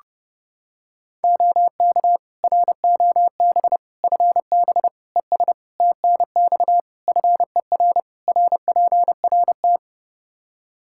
Morse code, heard as a keyed tone, about 30 wpm, 700 Hz. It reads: OK ROB FB ES TNX FER RPRT
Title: Practice QSO - Exchange 2 - Info - 30wpm